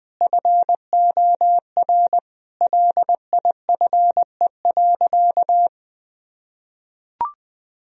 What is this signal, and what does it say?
Morse code, heard as a keyed tone, about 20 wpm, 700 Hz. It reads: FOR LIFE.
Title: Sentences from Top 1000 Words - Review Part 4 - 20wpm